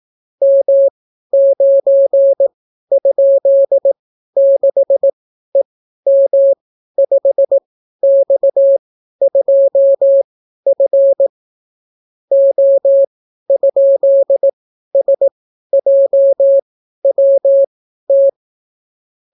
Morse code, heard as a keyed tone, about 18 wpm, spaced as for 12 wpm, 550 Hz. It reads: M9?6EM5X2F O?SJWT